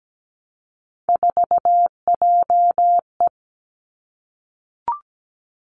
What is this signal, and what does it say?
Morse code, heard as a keyed tone, about 17 wpm, 700 Hz. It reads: 4JE
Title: Three Challenging Letter-Number Combinations 17wpm